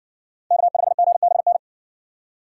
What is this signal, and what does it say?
Morse code, heard as a keyed tone, about 57 wpm, 700 Hz. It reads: K5GBN